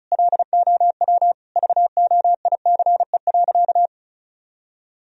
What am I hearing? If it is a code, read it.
Morse code, heard as a keyed tone, about 36 wpm, 700 Hz. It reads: LOW VOICE.
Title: Sentences from Top 1000 Words - Review Part 1 - 35wpm